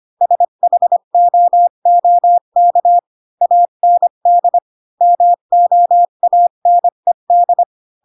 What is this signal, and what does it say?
Morse code, heard as a keyed tone, about 25 wpm, 700 Hz. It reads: SHOOK AND MOANED
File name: Pinocchio / Pinocchio ch6 20 25 ik1hge